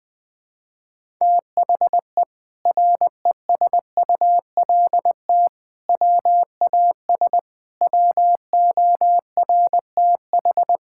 Morse code, heard as a keyed tone, about 21 wpm, 700 Hz. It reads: THE RESULT WAS WORTH
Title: Sentences from Top 1000 Words - Review Part 5 - 20wpm